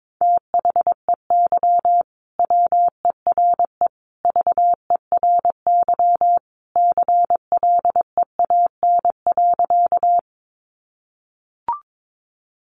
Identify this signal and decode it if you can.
Morse code, heard as a keyed tone, about 22 wpm, 700 Hz. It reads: THEY WERE VERY CLEAN.
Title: Sentences from Top 1100 Words 22wpm